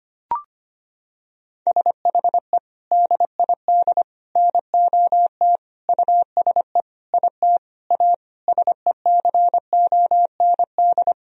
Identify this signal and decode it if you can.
Morse code, heard as a keyed tone, about 25 wpm, 700 Hz. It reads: SHE DID NOT USE IT A SECOND
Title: Sentences from Top 1000 Words - Review Part 1 - 25wpm